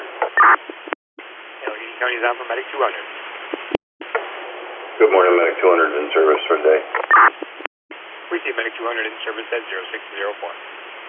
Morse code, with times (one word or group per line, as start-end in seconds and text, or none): none